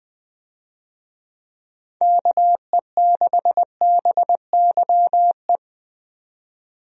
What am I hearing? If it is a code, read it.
Morse code, heard as a keyed tone, about 20 wpm, 700 Hz. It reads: KE6BYE
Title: USA Call Signs - 6 Characters - 20wpm